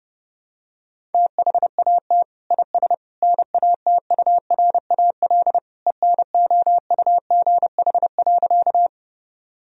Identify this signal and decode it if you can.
Morse code, heard as a keyed tone, about 30 wpm, 700 Hz. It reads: THAT IS NATURAL ENOUGH.
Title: Sentences from Top 1000 Words - Review Part 4 - 30wpm